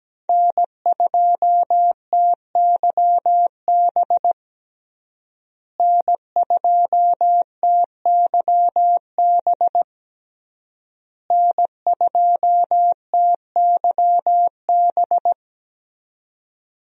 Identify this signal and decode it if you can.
Morse code, heard as a keyed tone, about 17 wpm, 700 Hz. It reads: N2TYB N2TYB N2TYB
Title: USA Call Signs - 5 Characters - Repeated 3x - 17wpm